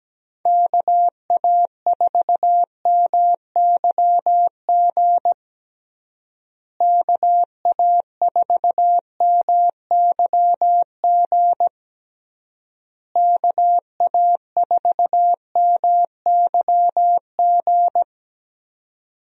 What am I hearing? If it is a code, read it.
Morse code, heard as a keyed tone, about 17 wpm, 700 Hz. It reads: KA4MYG KA4MYG KA4MYG
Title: USA Call Signs - 6 Characters - Repeated 3x - 17wpm